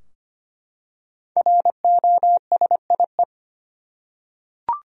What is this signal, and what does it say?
Morse code, heard as a keyed tone, about 25 wpm, 700 Hz. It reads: ROSIE
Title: USA Common Names - Max 5 Letters - 25wpm